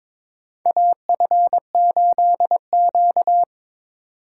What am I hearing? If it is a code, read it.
Morse code, heard as a keyed tone, about 22 wpm, 700 Hz. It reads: AF8Q
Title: USA Call Signs - 4x Character Spacing - 22wpm